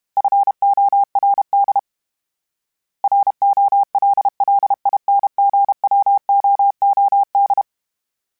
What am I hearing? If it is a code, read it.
Morse code, heard as a keyed tone, about 32 wpm, 800 Hz. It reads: FORD ROLLINGWOOD
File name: fromSMs / FromSMs mp3s / Cities@30